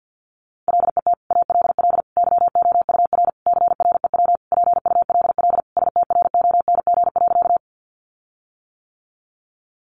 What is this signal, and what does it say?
Morse code, heard as a keyed tone, about 50 wpm, 700 Hz. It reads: LET ALL YOUR CREW PULL STRONG.
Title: Sentences from Top 1500 Words 50wpm